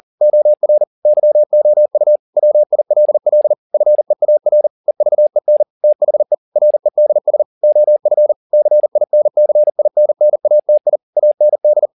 Morse code, heard as a keyed tone, about 40 wpm, 600 Hz. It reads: OR YOU WILL FEAR EVEN THE REDS OF CINCINNATI AND